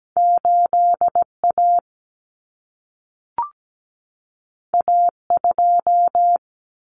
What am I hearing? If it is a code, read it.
Morse code, heard as a keyed tone, about 17 wpm, 700 Hz. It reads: 8A A2
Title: Common Cut Numbers - 2-Digit 17wpm